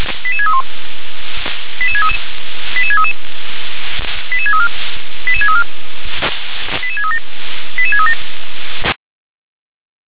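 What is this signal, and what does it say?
Five-tone selective call (ZVEI), heard as quick runs of five tones, tones of about 70 ms. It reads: 88631 88633 88633 88634 88634 88637 88637
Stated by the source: ZVEI five-tone selective call received via SDR.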